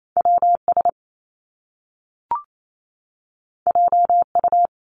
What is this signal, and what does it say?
Morse code, heard as a keyed tone, about 28 wpm, 700 Hz. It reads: WS JU
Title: Top Two Letter Combinations in Words 28wpm